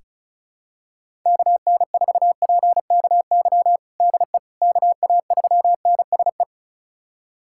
Morse code, heard as a keyed tone, about 34 wpm, 700 Hz. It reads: KN4PKY DE KA3NSE
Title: USA Callsign DE Callsign - 6 Characters - 35wpm